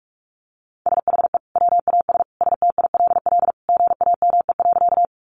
Morse code, heard as a keyed tone, about 44 wpm, 700 Hz. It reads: SHE WAS STILL GAME.
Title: Sentences from Top 500 Words 45wpm